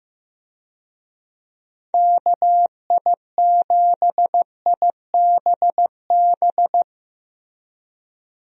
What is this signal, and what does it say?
Morse code, heard as a keyed tone, about 15 wpm, 700 Hz. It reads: KI7IBB
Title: USA Call Signs - 6 Characters - Repeated 2x - 15wpm